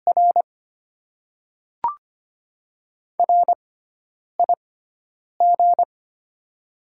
Morse code, heard as a keyed tone, about 25 wpm, spaced as for 9 wpm, 700 Hz. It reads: R RIG
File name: Top 20 QSO Elements - 6x Character Spacing - 25wpm